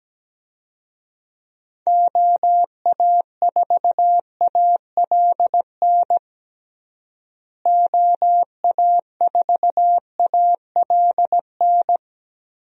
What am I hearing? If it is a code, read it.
Morse code, heard as a keyed tone, about 17 wpm, 700 Hz. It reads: OA4ALN OA4ALN
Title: USA Call Signs - 6 Characters - Repeated 2x - 17wpm